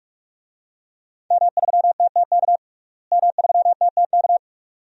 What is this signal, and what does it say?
Morse code, heard as a keyed tone, about 46 wpm, 700 Hz. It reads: M3TTK M3TTK
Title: DXCC-100 Call Signs - 5 Characters - Repeated 2x - 45wpm